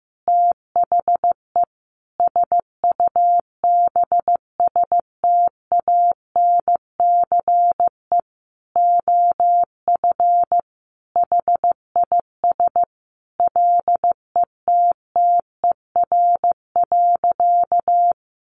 Morse code, heard as a keyed tone, about 15 wpm, 700 Hz. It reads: THE SUBSTANCE OF HIS LETTER.
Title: Sentences from Top 1500 Words 15wpm